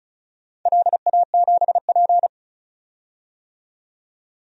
Morse code, heard as a keyed tone, about 34 wpm, 700 Hz. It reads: LA7P
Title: DXCC-100 Call Signs - 4 Characters - 35wpm